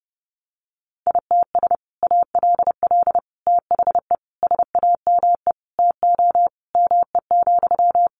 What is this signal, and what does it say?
Morse code, heard as a keyed tone, about 30 wpm, 700 Hz. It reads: ITS ALL THE SAME TO ME,
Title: Sentences from Top 600 Words 30wpm